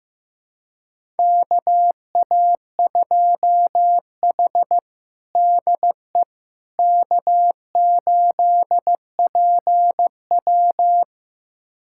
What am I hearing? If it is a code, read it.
Morse code, heard as a keyed tone, about 15 wpm, 700 Hz. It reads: KA2H DE K8PW
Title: USA Callsign DE Callsign - 4 Characters - 15wpm